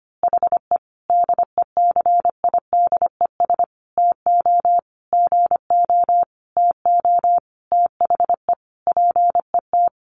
Morse code, heard as a keyed tone, about 25 wpm, 700 Hz. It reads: HE DECIDES TO GO TO THE PET
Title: Easy Phrases 25wpm